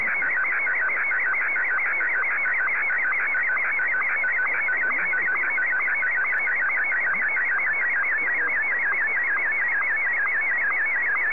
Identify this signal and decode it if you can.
SSTV (slow-scan television), Martin 1, with speech.
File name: sstv / M1-VK3